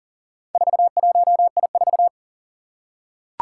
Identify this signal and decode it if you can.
Morse code, heard as a keyed tone, about 40 wpm, 700 Hz. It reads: 41I4
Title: Four Challenging Letter-Number Combinations 40wpm